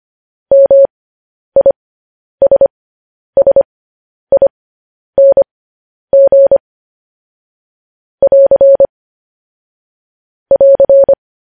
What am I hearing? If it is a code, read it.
Morse code, heard as a keyed tone, about 25 wpm, spaced as for 10 wpm, 550 Hz. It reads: MISSING <AR> <AR>
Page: news-headlines-in-morse-code-at-10-wpm